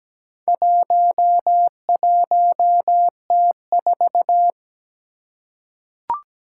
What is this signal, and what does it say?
Morse code, heard as a keyed tone, about 17 wpm, 700 Hz. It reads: 11T4
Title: Four Challenging Letter-Number Combinations 17wpm